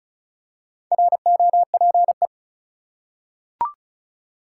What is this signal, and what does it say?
Morse code, heard as a keyed tone, about 36 wpm, 700 Hz. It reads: ROPE
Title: Four Letter Words - 35wpm